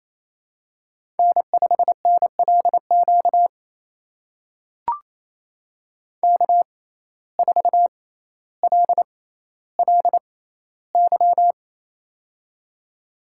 Morse code, heard as a keyed tone, about 27 wpm, spaced as for 14 wpm, 700 Hz. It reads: N5NLQ K4LLY
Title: USA Call Signs - 5 Characters - 6x Character Spacing - 28wpm